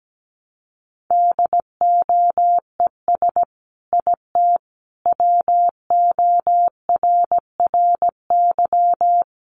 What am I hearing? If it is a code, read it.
Morse code, heard as a keyed tone, about 17 wpm, 700 Hz. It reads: DOES IT WORRY
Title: Sentences from Top 1100 Words 17wpm